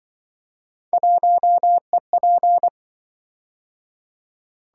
Morse code, heard as a keyed tone, about 26 wpm, 700 Hz. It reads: 1EP